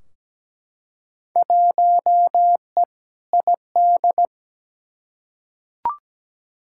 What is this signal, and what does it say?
Morse code, heard as a keyed tone, about 17 wpm, 700 Hz. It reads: 1E ID
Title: Field Day Exchanges - 17wpm